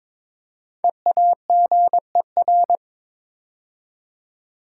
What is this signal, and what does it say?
Morse code, heard as a keyed tone, about 22 wpm, 690 Hz. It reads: EAGER
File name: 5 Letter Words - First 3 Letters Given - 22wpm